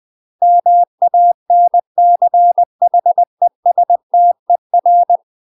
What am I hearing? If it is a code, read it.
Morse code, heard as a keyed tone, about 20 wpm, 700 Hz. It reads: MANCHESTER
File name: UK MASTER QSOs TRAINING / 20WPM / UK MASTER QSOs 0066